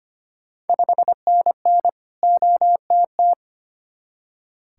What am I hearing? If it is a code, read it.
Morse code, heard as a keyed tone, about 25 wpm, 700 Hz. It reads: 5NN OTT
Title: Ontario QSO Party Exchange 25wpm